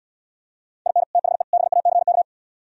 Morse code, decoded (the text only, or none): IN THE BACK